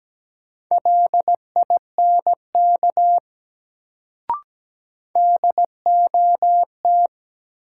Morse code, heard as a keyed tone, about 17 wpm, 700 Hz. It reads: LINK DOT
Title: Sentences from Top 1500 Words - New Word Drill Alt - 17wpm